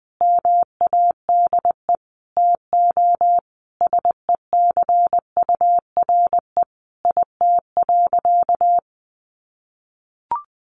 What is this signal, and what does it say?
Morse code, heard as a keyed tone, about 20 wpm, 700 Hz. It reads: MADE TO SECURE IT.
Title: Sentences from Top 1200 Words 20wpm